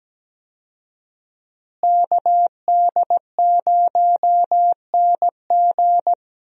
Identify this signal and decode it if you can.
Morse code, heard as a keyed tone, about 17 wpm, 700 Hz. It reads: KD0NG
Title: USA Call Signs - 5 Characters - Repeated 3x - 17wpm